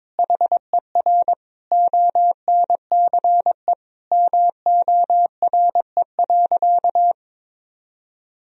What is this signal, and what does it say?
Morse code, heard as a keyed tone, about 22 wpm, 700 Hz. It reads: HER ONCE MORE.
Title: Sentences from Top 500 Words and Abbreviations 22wpm